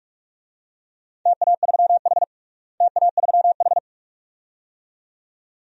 Morse code, heard as a keyed tone, about 46 wpm, 700 Hz. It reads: TA3H TA3H
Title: DXCC-100 Call Signs - 4 Characters - Repeated 2x - 45wpm